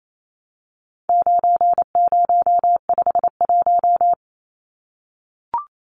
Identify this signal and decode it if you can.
Morse code, heard as a keyed tone, about 28 wpm, 700 Hz. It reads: 9051
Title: Numbers - 4-Digit 28wpm